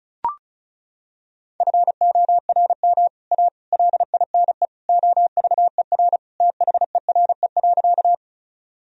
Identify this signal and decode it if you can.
Morse code, heard as a keyed tone, about 36 wpm, 700 Hz. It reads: FORM A LINE OVER THERE.
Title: Sentences from Top 1000 Words - Review Part 1 - 35wpm